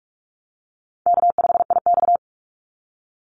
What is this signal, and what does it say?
Morse code, heard as a keyed tone, about 44 wpm, 700 Hz. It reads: K5IX
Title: Easy Call Signs Repeated 2x - 45wpm